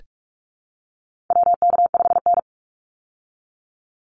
Morse code, heard as a keyed tone, about 44 wpm, 700 Hz. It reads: WK5N
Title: Call Signs Repeated 3x - 45wpm